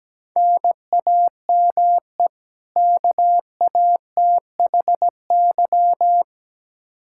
Morse code, heard as a keyed tone, about 17 wpm, 700 Hz. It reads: NAME KATHY
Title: Practice QSO - Exchange 1 - Beginning - 17wpm